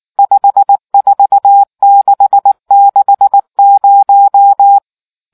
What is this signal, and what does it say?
Morse code, heard as a keyed tone, about 19 wpm, 800 Hz. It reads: 54660